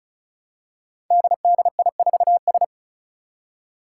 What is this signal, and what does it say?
Morse code, heard as a keyed tone, about 36 wpm, 700 Hz. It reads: DDI4S
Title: Five Challenging Letter-Number Combinations 35wpm